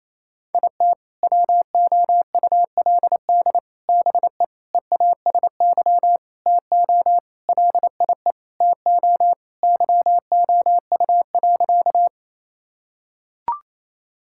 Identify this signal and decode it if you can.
Morse code, heard as a keyed tone, about 27 wpm, 700 Hz. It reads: IT WOULD BE EASY TO LIE TO YOU.
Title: Sentences from Top 1600 Words 28wpm